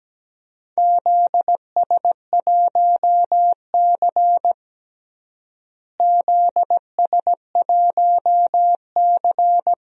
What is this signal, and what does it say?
Morse code, heard as a keyed tone, about 17 wpm, 700 Hz. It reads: ZS1C ZS1C
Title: DXCC-100 Call Signs - 4 Characters - Repeated 3x - 17wpm